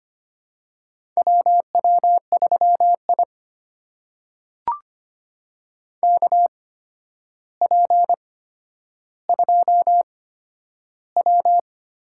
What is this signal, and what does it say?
Morse code, heard as a keyed tone, about 25 wpm, spaced as for 11 wpm, 700 Hz. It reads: WW3I KP2W
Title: USA Call Signs - 4 Characters - 8x Character Spacing - 25wpm